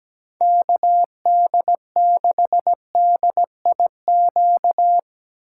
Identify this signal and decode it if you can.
Morse code, heard as a keyed tone, about 20 wpm, 700 Hz. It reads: KD6DIQ
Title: Speed Racing - USA Call Signs 17wpm